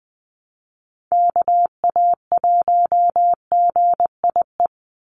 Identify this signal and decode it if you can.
Morse code, heard as a keyed tone, about 20 wpm, 700 Hz. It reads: KA1GIE
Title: USA Call Signs 20wpm